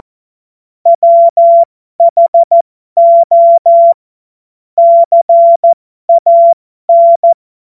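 Morse code, heard as a keyed tone, about 14 wpm, spaced as for 12 wpm, 680 Hz. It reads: WHO CAN
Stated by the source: Morse code joke practice.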